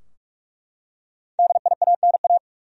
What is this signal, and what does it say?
Morse code, heard as a keyed tone, about 44 wpm, 700 Hz. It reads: DIANA
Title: USA Common Names - Max 5 Letters - 45wpm